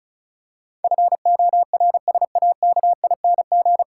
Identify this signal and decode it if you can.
Morse code, heard as a keyed tone, about 34 wpm, 700 Hz. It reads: FORSAKING
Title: Pangram Challenge v2 - 35wpm